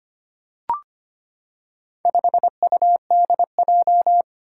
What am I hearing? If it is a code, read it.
Morse code, heard as a keyed tone, about 25 wpm, 700 Hz. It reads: 5UDJ
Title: Four Challenging Letter-Number Combinations 25wpm